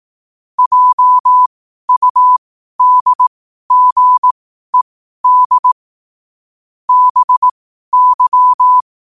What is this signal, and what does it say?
Morse code, heard as a keyed tone, about 18 wpm, spaced as for 12 wpm, 1000 Hz. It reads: JUDGED BY
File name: qotdfiles / 12wpm